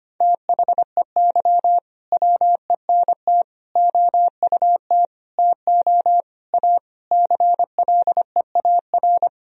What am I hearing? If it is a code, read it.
Morse code, heard as a keyed tone, about 25 wpm, 700 Hz. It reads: THEY WENT OUT TO A CLEAR
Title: Sentences from Top 1000 Words - Review Part 2 - 25wpm